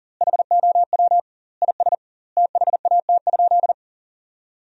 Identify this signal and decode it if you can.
Morse code, heard as a keyed tone, about 37 wpm, 700 Hz. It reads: HOW IS THAT?
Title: 2x Word-Spacing - Sentences from Top 100 Words - 40wpm